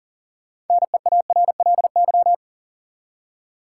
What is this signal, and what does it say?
Morse code, heard as a keyed tone, about 40 wpm, 700 Hz. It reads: NEARLY
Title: Six Letter Words - 40wpm